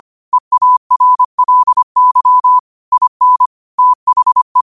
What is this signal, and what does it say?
Morse code, heard as a keyed tone, about 25 wpm, 1000 Hz. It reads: EARLY IN THE
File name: qotdfiles / 25wpm